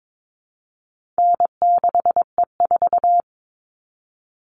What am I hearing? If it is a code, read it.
Morse code, heard as a keyed tone, about 22 wpm, 700 Hz. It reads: N6E4
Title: All Cut Numbers - 4-Digit 22wpm